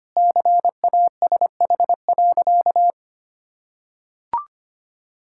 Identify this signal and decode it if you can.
Morse code, heard as a keyed tone, about 25 wpm, 700 Hz. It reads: CASH.
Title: Sentences from Top 1000 Words - Review Part 5 - 25wpm